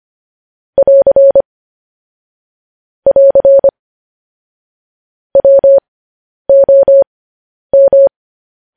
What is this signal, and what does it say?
Morse code, heard as a keyed tone, about 25 wpm, spaced as for 10 wpm, 550 Hz. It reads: <AR> <AR> WOM
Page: news headlines in morse code at 10 wpm